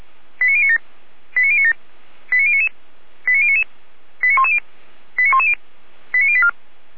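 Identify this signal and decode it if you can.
Five-tone selective call (ZVEI), heard as quick runs of five tones, tones of about 70 ms. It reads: 89097 89097 89099 89099 89110 89110 89084
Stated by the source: ZVEI five-tone selective call received on an SDR.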